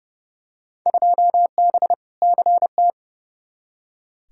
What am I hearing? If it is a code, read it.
Morse code, heard as a keyed tone, about 31 wpm, 700 Hz. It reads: 2B CT
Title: Field Day Exchanges Encore - 30wpm